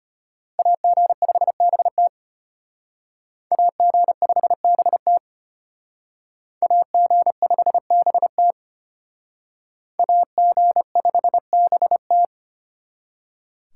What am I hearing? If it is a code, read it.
Morse code, heard as a keyed tone, about 33 wpm, 700 Hz. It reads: AG5BT AG5BT AG5BT AG5BT